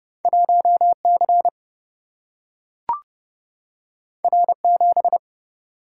Morse code, heard as a keyed tone, about 30 wpm, 700 Hz. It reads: 1C R7
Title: Single-Letter Single-Number 30wpm